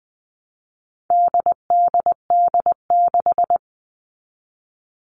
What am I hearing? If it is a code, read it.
Morse code, heard as a keyed tone, about 20 wpm, 700 Hz. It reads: DDD6
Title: All Cut Numbers - 4-Digit 20wpm